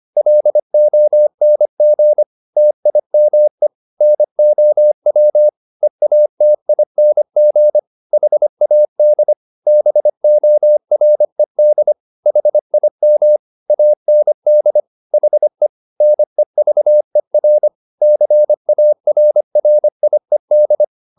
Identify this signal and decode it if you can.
Morse code, heard as a keyed tone, about 25 wpm, 600 Hz. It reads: LONG TIME NOW EATING HAD BORED HIM AND HE NEVER CARRIED
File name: the old man and the sea part 1 25wpm